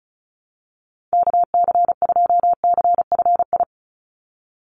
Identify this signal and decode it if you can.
Morse code, heard as a keyed tone, about 36 wpm, 700 Hz. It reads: KC2CFI
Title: USA Call Signs 35wpm